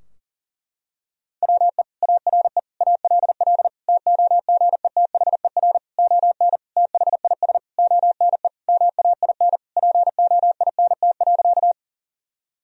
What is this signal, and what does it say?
Morse code, heard as a keyed tone, about 40 wpm, 700 Hz. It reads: WE ARE ALL TOGETHER ON THIS ONE MAIN POINT.
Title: Sentences from Top 1000 Words - Review Part 2 - 40wpm